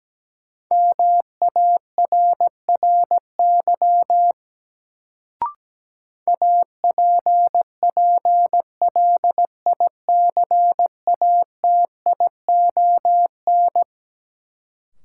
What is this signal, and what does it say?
Morse code, heard as a keyed tone, about 17 wpm, 700 Hz. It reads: MARRY APPLICATION
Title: Sentences from Top 1200 Words - New Word Drill - 17wpm